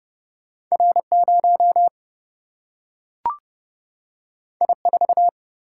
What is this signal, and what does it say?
Morse code, heard as a keyed tone, about 30 wpm, 700 Hz. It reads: R0 I4
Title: Single-Letter Single-Number 30wpm